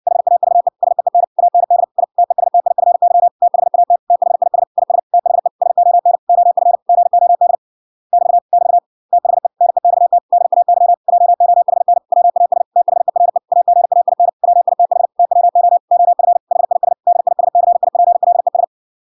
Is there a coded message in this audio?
Yes: Morse code from 0.1 to 18.6 s, heard as a keyed tone, about 58 wpm, 700 Hz. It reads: HAVE SEEN AND I TESTIFY THAT THIS IS THE SON OF GOD <BT> <BT> THE NEXT DAY JOHN WAS THERE AGAIN WITH TWO OF HIS DISCIPLES